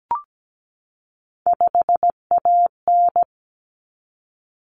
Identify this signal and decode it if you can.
Morse code, heard as a keyed tone, about 17 wpm, 700 Hz. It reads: 5AN
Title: Common Cut Numbers - 3-Digit 17wpm